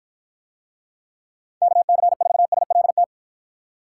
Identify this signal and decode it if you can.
Morse code, heard as a keyed tone, about 53 wpm, 700 Hz. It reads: KC4SLT